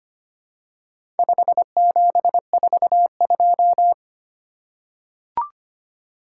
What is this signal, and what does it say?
Morse code, heard as a keyed tone, about 25 wpm, 700 Hz. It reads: 5742
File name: practice / 25wpm / Four Letter-Number Combinations - 25wpm